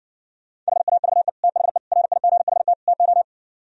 Morse code, heard as a keyed tone, about 68 wpm, 700 Hz. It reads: HAVE THE RIGHT TO